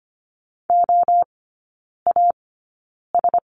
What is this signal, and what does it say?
Morse code, heard as a keyed tone, about 25 wpm, spaced as for 9 wpm, 700 Hz. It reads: OAS
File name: Lesson 17e - Letter u - Three Characters 25wpm